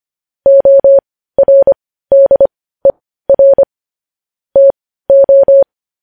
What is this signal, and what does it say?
Morse code, heard as a keyed tone, about 25 wpm, spaced as for 15 wpm, 550 Hz. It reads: ORDER TO
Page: news-headlines-in-morse-code-at-15-wpm